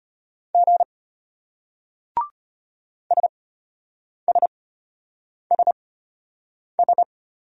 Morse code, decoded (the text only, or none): G SSSS